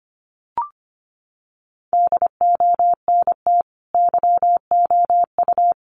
Morse code, heard as a keyed tone, about 25 wpm, 700 Hz. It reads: DONT YOU
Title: Sentences from Top 600 Words 25wpm